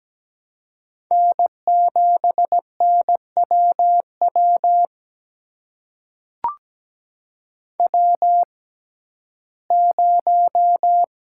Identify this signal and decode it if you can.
Morse code, heard as a keyed tone, about 17 wpm, spaced as for 12 wpm, 700 Hz. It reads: N7NWW W0
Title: USA Call Signs - 5 Characters - 6x Character Spacing - 17wpm